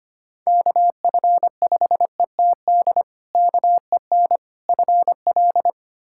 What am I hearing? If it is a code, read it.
Morse code, heard as a keyed tone, about 25 wpm, 700 Hz. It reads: KF5ETD KEN FL
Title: K1USN SST QSO Practice - 25wpm